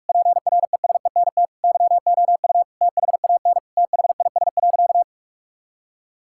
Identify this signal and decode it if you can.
Morse code, heard as a keyed tone, about 46 wpm, 700 Hz. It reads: PRESENT YOU THAN THIS.
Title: Sentences from Top 1000 Words - Review Part 6 - 45wpm